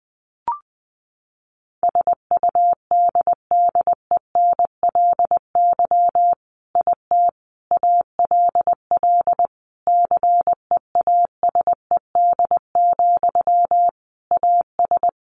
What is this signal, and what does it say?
Morse code, heard as a keyed tone, about 20 wpm, 700 Hz. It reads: SUDDENLY IT ALL CEASED, AS
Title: A Sound Of Thunder 20wpm